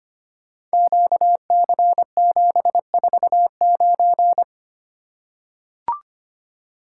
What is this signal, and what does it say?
Morse code, heard as a keyed tone, about 25 wpm, 700 Hz. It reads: QC749
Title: Code Groups - 5 Letter:Numbers - 25wpm